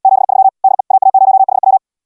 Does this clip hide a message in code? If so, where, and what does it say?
Yes: Morse code from 0.1 to 1.8 s, heard as a keyed tone, about 80 wpm, 770 Hz. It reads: 73 DE AA0HW